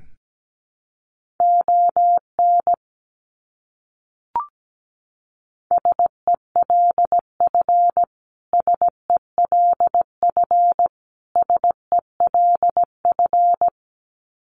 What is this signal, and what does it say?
Morse code, heard as a keyed tone, about 17 wpm, 700 Hz. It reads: ON SELF SELF SELF
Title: Top 300 Words Repeated 17wpm